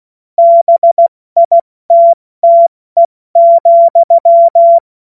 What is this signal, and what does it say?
Morse code, heard as a keyed tone, about 16 wpm, 680 Hz. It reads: BITTE,
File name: Morsekurs / Witze / Witz 05. April 2022 70 F80